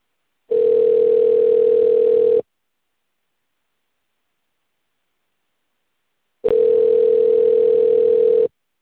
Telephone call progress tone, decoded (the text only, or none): ringback tone